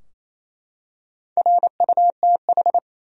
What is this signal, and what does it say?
Morse code, heard as a keyed tone, about 27 wpm, 700 Hz. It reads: RUTH
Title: USA Common Names - Max 4 Letters - 28wpm